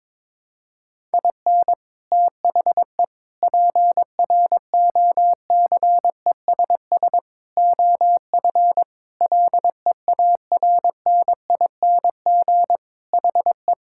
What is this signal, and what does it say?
Morse code, heard as a keyed tone, about 22 wpm, 700 Hz. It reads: IN THE PROCESS OF LEARNING HE
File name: practice / 22wpm / Sentences with 4-Word Phrases - 22wpm